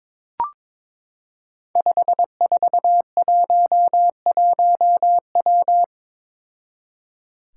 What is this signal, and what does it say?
Morse code, heard as a keyed tone, about 22 wpm, 700 Hz. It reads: 5411W
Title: Five Challenging Letter-Number Combinations 22wpm